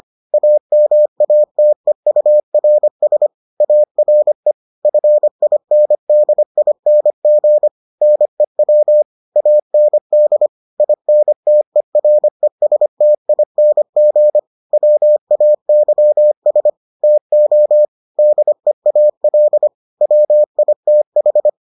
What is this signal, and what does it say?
Morse code, heard as a keyed tone, about 25 wpm, 600 Hz. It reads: AMATEURS ARE FINDING NEW AND INTERESTING WAYS TO DEAL WITH